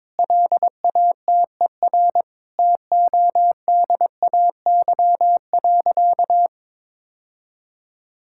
Morse code, heard as a keyed tone, about 22 wpm, 700 Hz. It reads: LATER TODAY.
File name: practice / 22wpm / Sentences with 4-Word Phrases - 22wpm